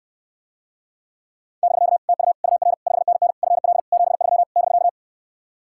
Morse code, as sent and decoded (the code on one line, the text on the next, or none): -...-.- / - ..- / ..- .-. / ..... -. -. / ..-. .-.. / --... ...-- / -...-.-
<BK> TU UR 5NN FL 73 <BK>